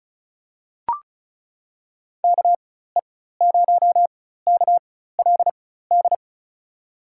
Morse code, heard as a keyed tone, about 34 wpm, spaced as for 16 wpm, 700 Hz. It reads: KE0KLD
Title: USA Call Signs - 6 Characters - 4x Character Spacing - 35wpm